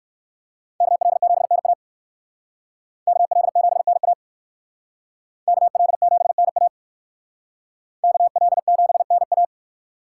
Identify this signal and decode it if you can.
Morse code, heard as a keyed tone, about 57 wpm, 700 Hz. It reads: KL7NA KL7NA KL7NA KL7NA